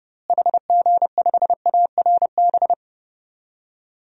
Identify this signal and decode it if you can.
Morse code, heard as a keyed tone, about 30 wpm, 700 Hz. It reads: HG5ARB
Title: DXCC-100 Call Signs - 6 Characters - Repeated 2x - 30wpm